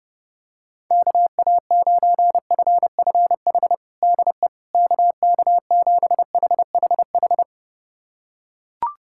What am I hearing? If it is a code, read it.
Morse code, heard as a keyed tone, about 32 wpm, 700 Hz. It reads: KA9FFH DE KK7HHH
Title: USA Callsign DE Callsign - 6 Characters - 30wpm